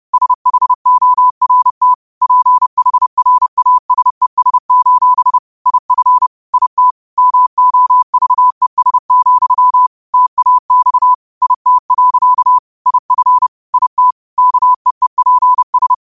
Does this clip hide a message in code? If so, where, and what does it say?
Yes: Morse code from 0.1 to 15.9 s, heard as a keyed tone, about 30 wpm, 1000 Hz. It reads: SHORT PHRASES: IF IT MOVES, TAX IT. IF IT KEEPS